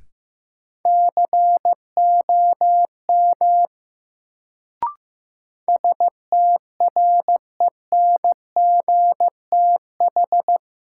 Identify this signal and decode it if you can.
Morse code, heard as a keyed tone, about 15 wpm, 700 Hz. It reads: COM STRENGTH
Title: Sentences from Top 1300 Words - New Word Drill Alt - 15wpm